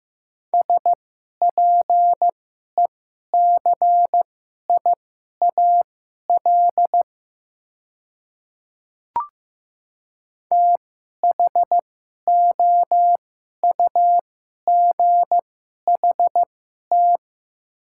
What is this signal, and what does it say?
Morse code, heard as a keyed tone, about 15 wpm, spaced as for 11 wpm, 700 Hz. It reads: SPECIAL THOUGHT
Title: 2x Character Spacing - Top 500 Words - 15wpm